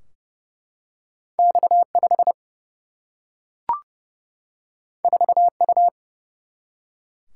Morse code, heard as a keyed tone, about 30 wpm, 700 Hz. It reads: X5 4U
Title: Single-Letter Single-Number 30wpm